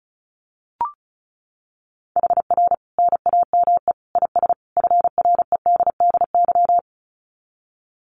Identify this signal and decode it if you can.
Morse code, heard as a keyed tone, about 34 wpm, 700 Hz. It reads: HR NAME IS FREDDY
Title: Common QSO Phrases 35wpm